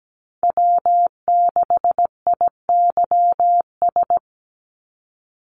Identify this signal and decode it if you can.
Morse code, heard as a keyed tone, about 17 wpm, 700 Hz. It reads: W6IYS
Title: NA SOTA Chaser Callsigns 17wpm